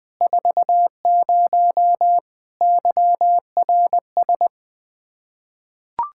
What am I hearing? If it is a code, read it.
Morse code, heard as a keyed tone, about 20 wpm, 700 Hz. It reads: 40 YRS
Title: Practice QSO - Exchange 3 - More Info - 20wpm